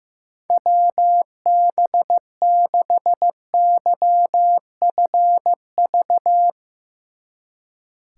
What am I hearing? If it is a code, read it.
Morse code, heard as a keyed tone, about 15 wpm, 700 Hz. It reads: WB6YFV
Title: Speed Racing - USA 2x3 Call Signs 15wpm